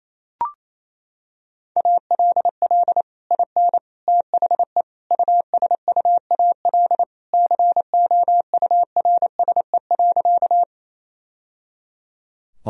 Morse code, heard as a keyed tone, about 27 wpm, 700 Hz. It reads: ALL IN THE USUAL COURSE.
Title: Sentences from Top 1000 Words - Review Part 2 - 28wpm